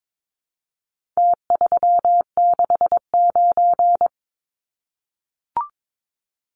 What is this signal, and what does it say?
Morse code, heard as a keyed tone, about 22 wpm, 700 Hz. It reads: T369
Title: Numbers - Contest Style - 4-Digit 22wpm